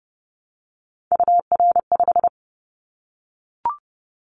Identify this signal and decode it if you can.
Morse code, heard as a keyed tone, about 30 wpm, 700 Hz. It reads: UR5
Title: Callsign Prefixes for Top 25 DXCC - Repeated 3x - 30wpm